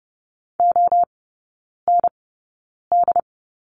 Morse code, heard as a keyed tone, about 30 wpm, spaced as for 9 wpm, 700 Hz. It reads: OND
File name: Lesson 17e - Letter u - Three Characters 30wpm